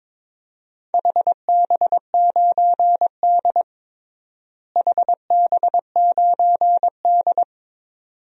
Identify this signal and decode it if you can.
Morse code, heard as a keyed tone, about 22 wpm, 700 Hz. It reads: HB9D HB9D
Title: DXCC-100 Call Signs - 4 Characters - Repeated 2x - 22wpm